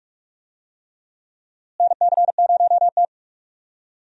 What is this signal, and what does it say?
Morse code, heard as a keyed tone, about 42 wpm, 700 Hz. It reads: NC0T
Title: USA Call Signs - 4 Characters - Repeated 3x - 45wpm